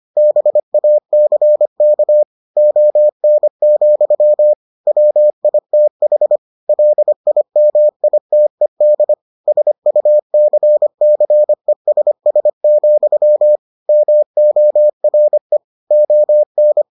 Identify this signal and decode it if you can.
Morse code, heard as a keyed tone, about 25 wpm, 600 Hz. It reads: BACK ON, WITH LIMITED SUCCESS, MORE ON